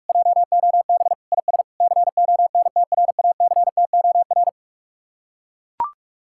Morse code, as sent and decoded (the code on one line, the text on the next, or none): .--- --- -... / .. ... / -.-. --- -. - .-. .- -.-. - --- .-.
JOB IS CONTRACTOR